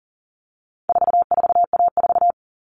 Morse code, heard as a keyed tone, about 40 wpm, 700 Hz. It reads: 44A4
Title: All Cut Numbers - 4-Digit 40wpm